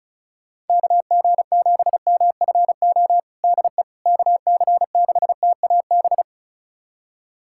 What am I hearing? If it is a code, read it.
Morse code, heard as a keyed tone, about 36 wpm, 700 Hz. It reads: KG7MFO DE KC6TAB
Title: USA Callsign DE Callsign 35wpm